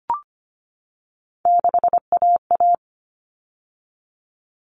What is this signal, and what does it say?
Morse code, heard as a keyed tone, about 25 wpm, 700 Hz. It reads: 6AA
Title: Common Cut Numbers - 3-Digit 25wpm